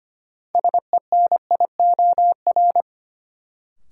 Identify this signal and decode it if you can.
Morse code, heard as a keyed tone, about 25 wpm, 700 Hz. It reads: SENIOR
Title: Sets of 5 Words Encore - New Word Drill - 25wpm